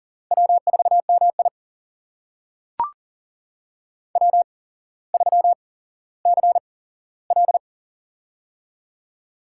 Morse code, heard as a keyed tone, about 42 wpm, spaced as for 17 wpm, 700 Hz. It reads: W4MI W3CL
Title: USA Call Signs - 4 Characters - 8x Character Spacing - 40wpm